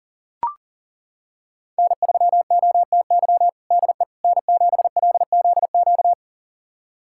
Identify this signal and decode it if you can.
Morse code, heard as a keyed tone, about 40 wpm, 700 Hz. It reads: N3OTY DE N7LZQ
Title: USA Callsign DE Callsign - 5 Characters - 40wpm